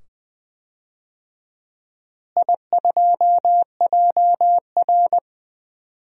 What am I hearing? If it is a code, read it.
Morse code, heard as a keyed tone, about 20 wpm, 700 Hz. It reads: I2JR
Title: DXCC-100 Call Signs - 4 Characters - Repeated 2x - 20wpm